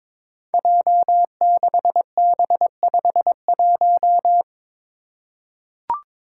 Morse code, heard as a keyed tone, about 22 wpm, 700 Hz. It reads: J6B51
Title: Five Challenging Letter-Number Combinations 22wpm